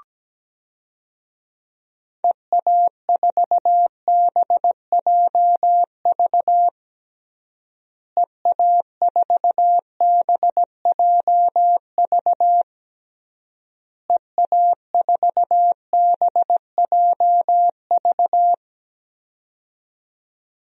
Morse code, heard as a keyed tone, about 17 wpm, 700 Hz. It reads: EA4BJV EA4BJV EA4BJV
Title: DXCC-100 Call Signs - 6 Characters - Repeated 3x - 17wpm